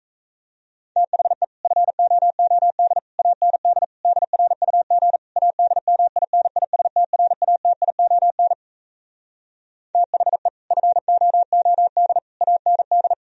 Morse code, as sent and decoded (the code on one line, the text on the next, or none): - .... / ..-. --- --- -.. / .- -. -.. / -.. .-. ..- --. / .- -.. -- .. -. .. ... - .-. .- - .. --- -. / - .... / ..-. --- --- -.. / .- -. -..
THE FOOD AND DRUG ADMINISTRATION THE FOOD AND